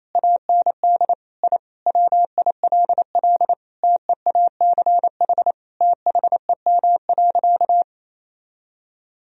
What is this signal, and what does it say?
Morse code, heard as a keyed tone, about 28 wpm, 700 Hz. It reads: AND I WILL TEACH THEM.
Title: Sentences from Top 1000 Words - Review Part 2 - 28wpm